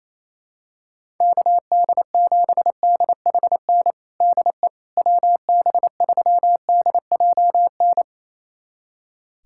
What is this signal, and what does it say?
Morse code, heard as a keyed tone, about 28 wpm, 700 Hz. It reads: KD7DHN DE WB3DJN
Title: USA Callsign DE Callsign 28wpm